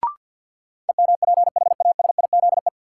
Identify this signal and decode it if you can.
Morse code, heard as a keyed tone, about 53 wpm, 700 Hz. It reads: EMPHASIZE
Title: Sentences from Top 1500 Words - New Word Drill - 50wpm